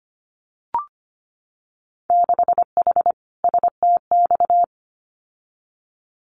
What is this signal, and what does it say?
Morse code, heard as a keyed tone, about 25 wpm, 700 Hz. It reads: 6H STX
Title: Winter Field Day Exchanges Encore 25wpm